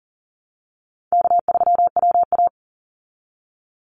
Morse code, heard as a keyed tone, about 40 wpm, 700 Hz. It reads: K3WA
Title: Easy Call Signs Repeated 2x - 40wpm